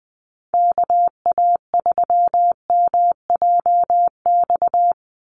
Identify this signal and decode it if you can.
Morse code, heard as a keyed tone, about 20 wpm, 700 Hz. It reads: KA3MJX
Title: USA Call Signs 20wpm